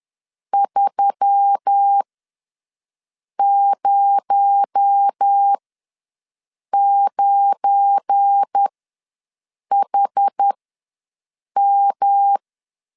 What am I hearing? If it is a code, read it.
Morse code, heard as a keyed tone, about 10 wpm, spaced as for 5 wpm, 800 Hz. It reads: 309HM
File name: LEZIONI / 7 GF9Z / 7 GF9 ws 10wpm 4034